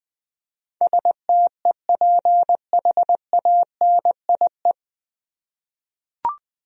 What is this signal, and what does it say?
Morse code, heard as a keyed tone, about 20 wpm, 700 Hz. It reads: STEPHANIE
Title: USA Common Names - 20wpm